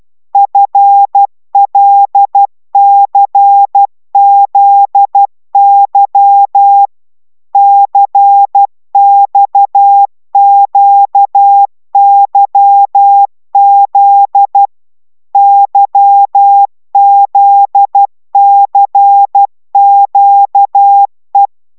Morse code, heard as a keyed tone, about 12 wpm, 800 Hz. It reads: FLCZY CXQYZ YZCQE
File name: taviro tanulas / cw 69 19 10